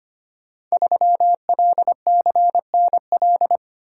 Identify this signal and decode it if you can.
Morse code, heard as a keyed tone, about 25 wpm, 700 Hz. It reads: 3LCNL
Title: Code Groups - 5 Letter:Numbers - 25wpm